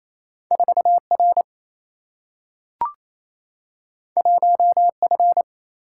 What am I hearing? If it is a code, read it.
Morse code, heard as a keyed tone, about 28 wpm, 700 Hz. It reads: 4R 1F
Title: Single-Letter Single-Number 28wpm